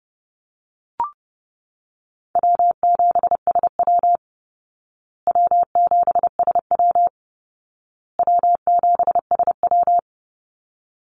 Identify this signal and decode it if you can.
Morse code, heard as a keyed tone, about 30 wpm, 700 Hz. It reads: W7SW W7SW W7SW
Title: Call Signs Repeated 2x - 30wpm